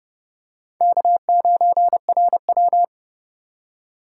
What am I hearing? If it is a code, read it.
Morse code, heard as a keyed tone, about 26 wpm, 700 Hz. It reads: K9RW